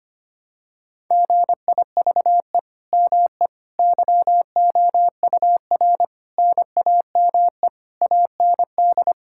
Morse code, heard as a keyed tone, about 25 wpm, 700 Hz. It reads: GIVE ME YOUR NAME AND
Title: Sentences from Top 1000 Words - Review Part 6 - 25wpm